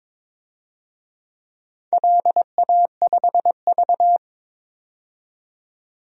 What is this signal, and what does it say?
Morse code, heard as a keyed tone, about 22 wpm, 700 Hz. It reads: LA5V
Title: DXCC-100 Call Signs - 4 Characters - 22wpm